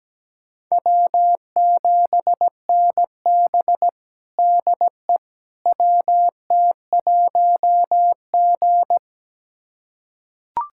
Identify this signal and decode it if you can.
Morse code, heard as a keyed tone, about 17 wpm, 700 Hz. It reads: W7NB DE WT1G
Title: USA Callsign DE Callsign - 4 Characters - 17wpm